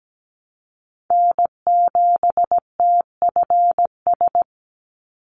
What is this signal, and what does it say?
Morse code, heard as a keyed tone, about 17 wpm, 700 Hz. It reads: N7TFS
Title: USA Call Signs 17wpm